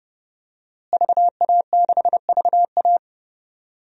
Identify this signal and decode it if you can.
Morse code, heard as a keyed tone, about 29 wpm, 700 Hz. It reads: VA6VA
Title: DXCC-100 Call Signs - 5 Characters - Repeated 3x - 30wpm